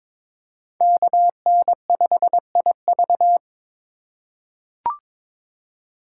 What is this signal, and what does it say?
Morse code, heard as a keyed tone, about 22 wpm, 700 Hz. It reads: KN5IV
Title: Code Groups - 5 Letter:Numbers - 22wpm